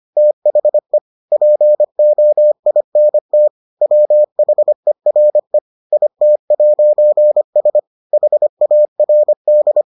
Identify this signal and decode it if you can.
Morse code, heard as a keyed tone, about 25 wpm, 600 Hz. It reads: THE POINT WHERE IT'S HARD